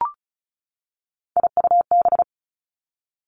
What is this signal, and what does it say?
Morse code, heard as a keyed tone, about 36 wpm, 700 Hz. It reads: IUB
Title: Three Challenging Letter-Number Combinations 35wpm